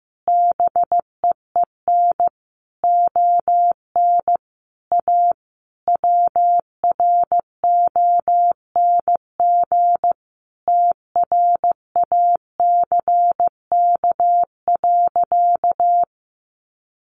Morse code, heard as a keyed tone, about 15 wpm, 700 Hz. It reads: BEEN ON A WRONG TRACK.
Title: Sentences from Top 1100 Words 15wpm